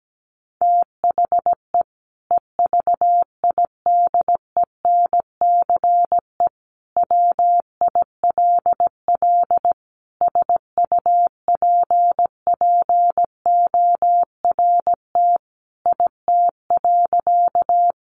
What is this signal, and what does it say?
Morse code, heard as a keyed tone, about 17 wpm, 700 Hz. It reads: THE EVIDENCE WILL SUPPORT IT.
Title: Sentences from Top 1300 Words 17wpm